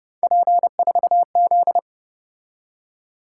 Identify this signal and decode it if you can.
Morse code, heard as a keyed tone, about 29 wpm, 700 Hz. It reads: P4Z